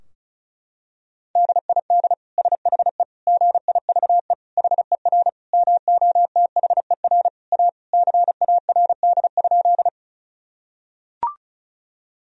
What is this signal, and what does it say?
Morse code, heard as a keyed tone, about 36 wpm, 700 Hz. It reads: DID SHE GIVE HER MOTHER A CARD?
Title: Sentences from Top 1000 Words - Review Part 4 - 35wpm